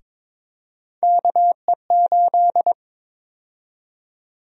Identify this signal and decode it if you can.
Morse code, heard as a keyed tone, about 22 wpm, 700 Hz. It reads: KE8
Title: USA Call Sign Prefixes - 22wpm